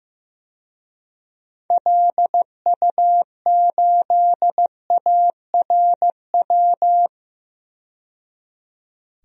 Morse code, heard as a keyed tone, about 15 wpm, 700 Hz. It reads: LU8ARW
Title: DXCC-100 Call Signs - 6 Characters - 15wpm